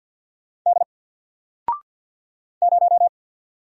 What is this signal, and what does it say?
Morse code, heard as a keyed tone, about 48 wpm, 700 Hz. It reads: D 0
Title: Single Letter-Number Alt 50wpm